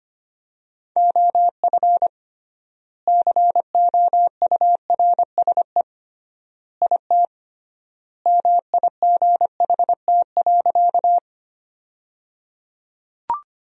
Morse code, heard as a keyed tone, about 25 wpm, 700 Hz. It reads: OF COURSE IT MIGHT.
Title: 3x Word-Spacing - Sentences from Top 500 Words - 25wpm